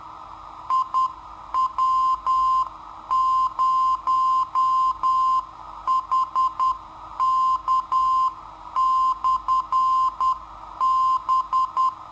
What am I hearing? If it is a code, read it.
Morse code, heard as a keyed tone, about 10 wpm, 1040 Hz. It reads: IW0HK/B
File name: sitoariold / beacon hk